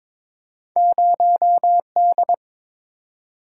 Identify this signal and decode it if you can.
Morse code, heard as a keyed tone, about 22 wpm, 700 Hz. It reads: 0D
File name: practice / 22wpm / Two Letter-Number Combinations - 22wpm